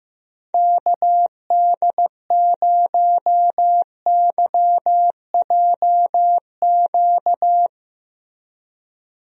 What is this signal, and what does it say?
Morse code, heard as a keyed tone, about 15 wpm, 700 Hz. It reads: KD0YJQ
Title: USA Call Signs - 6 Characters - Repeated 3x - 15wpm